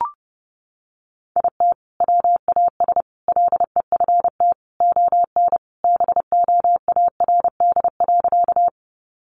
Morse code, heard as a keyed tone, about 30 wpm, 700 Hz. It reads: IT WAS LEFT ON BOARD.
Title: Sentences from Top 700 Words 30wpm